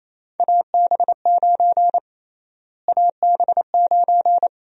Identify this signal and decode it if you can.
Morse code, heard as a keyed tone, about 28 wpm, 700 Hz. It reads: AB9 AB9
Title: USA Call Sign Prefixes - Repeated 2x - 28wpm